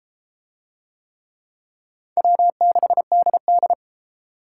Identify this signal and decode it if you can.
Morse code, heard as a keyed tone, about 31 wpm, 700 Hz. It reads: W6DD